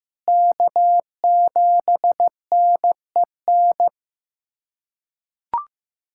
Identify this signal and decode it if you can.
Morse code, heard as a keyed tone, about 16 wpm, 700 Hz. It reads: K7NEN
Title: USA Callsign DE Callsign - 5 Characters - 15wpm